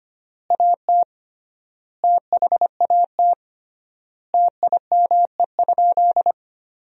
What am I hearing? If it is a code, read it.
Morse code, heard as a keyed tone, about 25 wpm, 700 Hz. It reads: AT THAT TIME?
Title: 3x Word-Spacing - Sentences from Top 200 Words - 25wpm